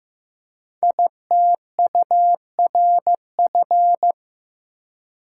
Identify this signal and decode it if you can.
Morse code, heard as a keyed tone, about 15 wpm, 700 Hz. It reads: ITURF
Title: Code Groups - 5 Letters - 15wpm